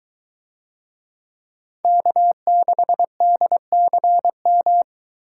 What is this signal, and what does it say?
Morse code, heard as a keyed tone, about 22 wpm, 700 Hz. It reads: K6DCM